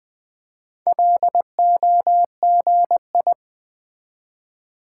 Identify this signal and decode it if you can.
Morse code, heard as a keyed tone, about 20 wpm, 700 Hz. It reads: LOGI
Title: Build A Word - 1 Letter At A Time - 5-Letter Words - 20wpm